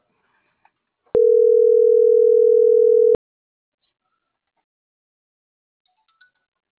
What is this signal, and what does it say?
Ringback tone (North America), 440 + 480 Hz.
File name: CENTRAL MULTIMODAL / out francisco.cobos